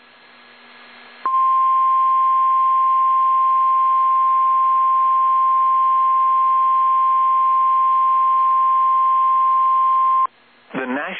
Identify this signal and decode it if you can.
Emergency Alert System (SAME) tones (the 1050 Hz attention tone), with speech. It is heard from 1.2 to 10.3 s.